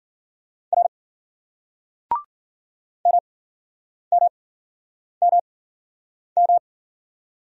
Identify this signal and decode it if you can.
Morse code, heard as a keyed tone, about 49 wpm, spaced as for 9 wpm, 700 Hz. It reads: R MMMM
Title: Speed-Racing Single Letter-Number 40wpm